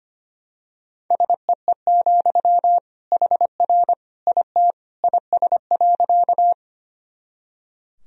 Morse code, heard as a keyed tone, about 25 wpm, 700 Hz. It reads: SEE, HR IT IS.
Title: Sentences from Top 200 Words and Abbreviations 25wpm